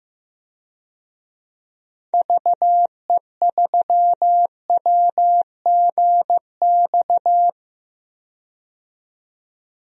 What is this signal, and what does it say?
Morse code, heard as a keyed tone, about 15 wpm, 700 Hz. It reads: VE3WGX
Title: DXCC-100 Call Signs - 6 Characters - 15wpm